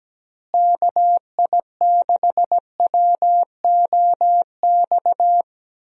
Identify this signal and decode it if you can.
Morse code, heard as a keyed tone, about 17 wpm, 700 Hz. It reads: KI6WOX
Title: USA Call Signs - 6 Characters - Repeated 2x - 17wpm